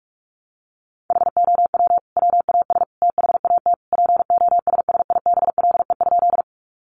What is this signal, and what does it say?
Morse code, heard as a keyed tone, about 44 wpm, 700 Hz. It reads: HOW WAS THAT POSSIBLE?
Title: Sentences from Top 500 Words 45wpm